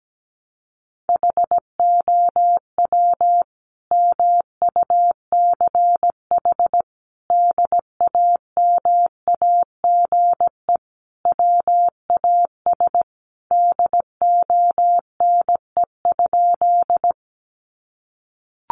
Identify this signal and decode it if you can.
Morse code, heard as a keyed tone, about 17 wpm, 700 Hz. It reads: HOW MUCH DAMAGE WAS DONE?
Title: Sentences from Top 1100 Words 17wpm